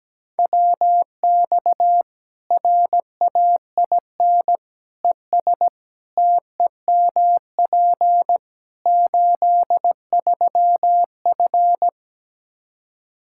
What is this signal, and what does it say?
Morse code, heard as a keyed tone, about 17 wpm, 700 Hz. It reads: WX RAIN ES TEMP 83F
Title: Practice QSO - Exchange 2 - Info - 17wpm